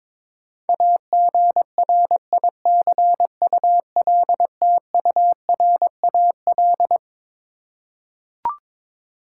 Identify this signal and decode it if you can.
Morse code, heard as a keyed tone, about 21 wpm, 700 Hz. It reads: AGRICULTURAL
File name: Twelve Letter Words - 22wpm